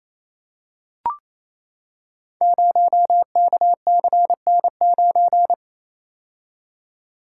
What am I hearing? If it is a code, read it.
Morse code, heard as a keyed tone, about 27 wpm, 700 Hz. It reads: 0KCN9
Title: Code Groups - 5 Letter:Numbers - 28wpm